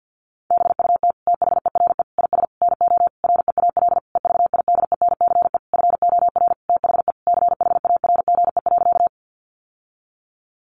Morse code, heard as a keyed tone, about 50 wpm, 700 Hz. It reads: BUT THERE IS NO REAL EVIDENCE FOR THE CHARGE.
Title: Sentences from Top 1300 Words 50wpm